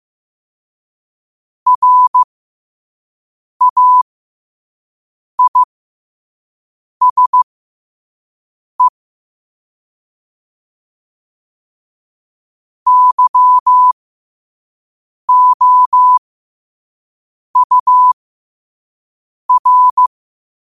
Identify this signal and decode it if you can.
Morse code, heard as a keyed tone, about 15 wpm, spaced as for 5 wpm, 1000 Hz. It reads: RAISE YOUR
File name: qotdfiles / 5wpm